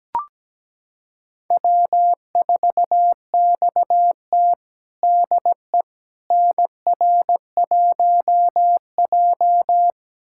Morse code, heard as a keyed tone, about 17 wpm, 700 Hz. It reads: W4XT DE NR1J
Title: USA Callsign DE Callsign - 4 Characters - 17wpm